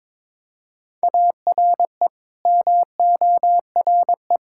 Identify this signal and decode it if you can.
Morse code, heard as a keyed tone, about 22 wpm, 700 Hz. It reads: ARE MORE
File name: practice / 22wpm / Build A Phrase - 5-Word Phrases - 22wpm